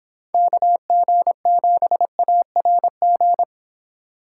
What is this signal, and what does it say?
Morse code, heard as a keyed tone, about 25 wpm, 700 Hz. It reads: KG7ARG